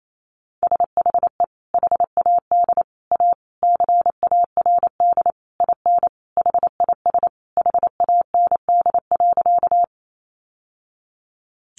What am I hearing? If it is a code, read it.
Morse code, heard as a keyed tone, about 28 wpm, 700 Hz. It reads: SHE HAD A CARD IN HIS HAND.
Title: Sentences from Top 600 Words 28wpm